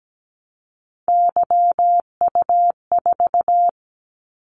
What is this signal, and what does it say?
Morse code, heard as a keyed tone, about 17 wpm, 700 Hz. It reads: YU4
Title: Callsign Prefixes for Top 25 DXCC - 17wpm